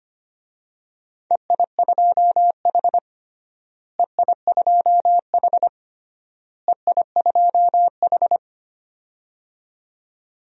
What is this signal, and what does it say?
Morse code, heard as a keyed tone, about 25 wpm, 700 Hz. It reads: EI2H EI2H EI2H
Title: DXCC-100 Call Signs - 4 Characters - Repeated 3x - 25wpm